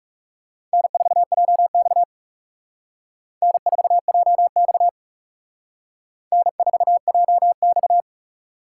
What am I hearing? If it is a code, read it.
Morse code, heard as a keyed tone, about 38 wpm, 700 Hz. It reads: N4JX N4JX N4JX
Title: Speed Racing - USA 1x2 Call Signs 30wpm